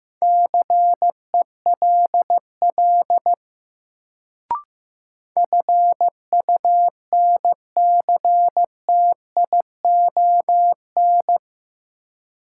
Morse code, heard as a keyed tone, about 15 wpm, 700 Hz. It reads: CELL FUNCTION
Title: Sentences from Top 1100 Words - New Word Drill Alt - 15wpm